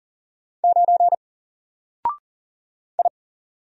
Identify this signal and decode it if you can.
Morse code, heard as a keyed tone, about 40 wpm, 700 Hz. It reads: 9 I
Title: Single Letter-Number Alt 40wpm